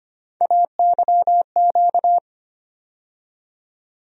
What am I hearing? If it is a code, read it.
Morse code, heard as a keyed tone, about 25 wpm, 700 Hz. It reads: AYQ
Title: USA Call Sign Postfixes - 3 Letters 25wpm